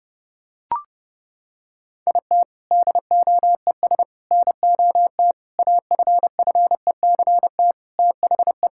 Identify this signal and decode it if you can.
Morse code, heard as a keyed tone, about 30 wpm, 700 Hz. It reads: IT DOES NOT AFFECT THE
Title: Sentences from Top 1000 Words - Review Part 6 - 30wpm